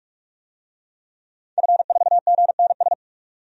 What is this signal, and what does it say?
Morse code, heard as a keyed tone, about 46 wpm, 700 Hz. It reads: F4GNS
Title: DXCC-100 Call Signs - 5 Characters - 45wpm